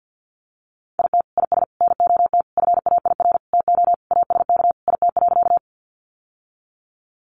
Morse code, heard as a keyed tone, about 49 wpm, 700 Hz. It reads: IT IS NOT FAIR TO ASK IT.
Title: Sentences from Top 1300 Words 50wpm